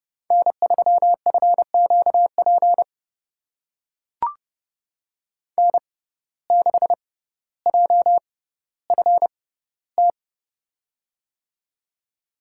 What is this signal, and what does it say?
Morse code, heard as a keyed tone, about 32 wpm, spaced as for 16 wpm, 700 Hz. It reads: N3FQP N6JFT
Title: USA Call Signs - 5 Characters - 6x Character Spacing - 30wpm